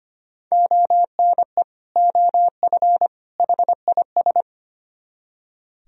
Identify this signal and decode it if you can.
Morse code, heard as a keyed tone, about 31 wpm, 700 Hz. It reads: ONE OF HIS
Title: Speed-Racing - Sets of 3 Words 25wpm